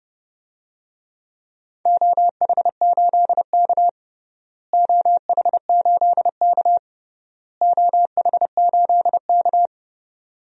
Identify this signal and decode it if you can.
Morse code, heard as a keyed tone, about 30 wpm, 700 Hz. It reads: OH8K OH8K OH8K
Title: DXCC-100 Call Signs - 4 Characters - Repeated 3x - 30wpm